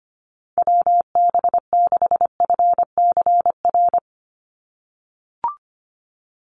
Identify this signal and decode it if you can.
Morse code, heard as a keyed tone, about 25 wpm, 700 Hz. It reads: WB6FCR
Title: USA Call Signs 25wpm